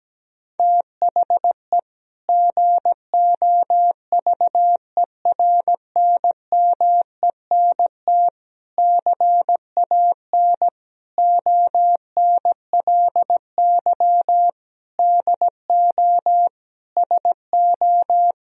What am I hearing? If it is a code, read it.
Morse code, heard as a keyed tone, about 17 wpm, 700 Hz. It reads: THE GOVERNMENT CAN ONLY DO SO
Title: Sentences from Top 1000 Words - Review Part 3 - 17wpm